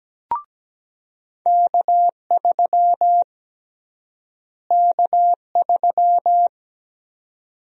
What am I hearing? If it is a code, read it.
Morse code, heard as a keyed tone, about 17 wpm, 700 Hz. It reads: K3 K3
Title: USA Call Sign Prefixes - Repeated 2x - 17wpm